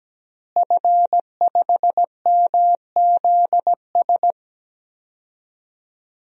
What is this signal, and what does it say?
Morse code, heard as a keyed tone, about 17 wpm, 700 Hz. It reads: F5MZS
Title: DXCC-100 Call Signs - 5 Characters - Repeated 3x - 17wpm